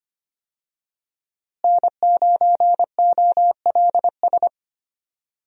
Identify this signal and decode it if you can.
Morse code, heard as a keyed tone, about 25 wpm, 700 Hz. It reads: N9OLS
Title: USA Call Signs - 5 Characters - Repeated 2x - 25wpm